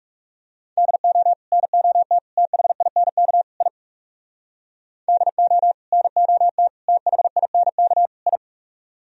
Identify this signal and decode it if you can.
Morse code, heard as a keyed tone, about 40 wpm, 700 Hz. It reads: DO NOT THINK I DO NOT THINK I